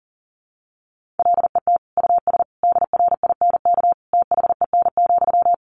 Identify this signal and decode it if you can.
Morse code, heard as a keyed tone, about 40 wpm, 700 Hz. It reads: LET US DRINK THEN,
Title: Sentences from Top 1500 Words 40wpm